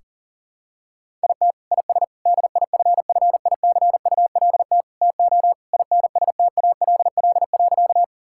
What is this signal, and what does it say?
Morse code, heard as a keyed tone, about 37 wpm, 700 Hz. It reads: IT IS DIFFICULT TO INSTALL.
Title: Sentences from Top 1600 Words 40wpm